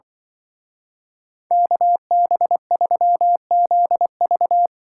Morse code, heard as a keyed tone, about 24 wpm, 700 Hz. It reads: KB3ZV
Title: Speed Racing - USA 2x2 Call Signs 20wpm